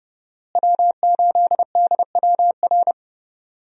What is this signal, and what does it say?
Morse code, heard as a keyed tone, about 30 wpm, 700 Hz. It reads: W8DWR
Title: USA Call Signs - 5 Characters - Repeated 3x - 30wpm